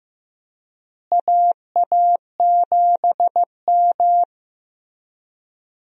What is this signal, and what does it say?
Morse code, heard as a keyed tone, about 15 wpm, 700 Hz. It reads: AA7M
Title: USA Call Signs - 4 Characters - Repeated 3x - 15wpm